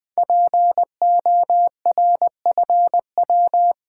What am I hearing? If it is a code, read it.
Morse code, heard as a keyed tone, about 20 wpm, 700 Hz. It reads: PORFW
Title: Code Groups - 5 Letters - 20wpm